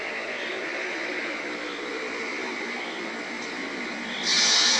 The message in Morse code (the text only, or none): none